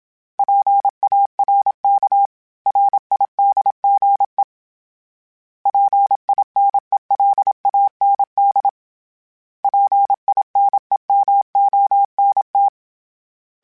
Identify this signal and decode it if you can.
Morse code, heard as a keyed tone, about 27 wpm, 800 Hz. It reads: PARK RIDGE PINELAND PINEMONT
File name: fromSMs / FromSMs mp3s / Cities@25